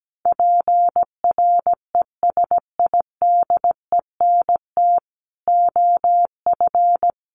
Morse code, heard as a keyed tone, about 17 wpm, 700 Hz. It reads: PRESIDENT OF
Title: Sets of 5 Words 17wpm